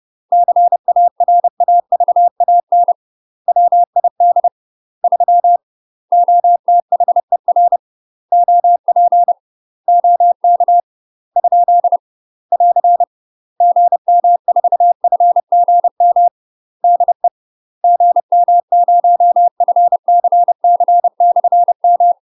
Morse code, heard as a keyed tone, about 30 wpm, 700 Hz. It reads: CARAVAN WID 3 OTHER OP OK ? <AR> GM4FGM DE GM0FCC/M
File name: UK MASTER QSOs TRAINING / 30WPM / UK MASTER QSOs 0051